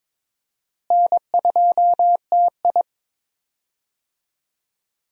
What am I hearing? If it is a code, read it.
Morse code, heard as a keyed tone, about 22 wpm, 700 Hz. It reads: N2TI
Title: USA Call Signs - 4 Characters - Repeated 2x - 22wpm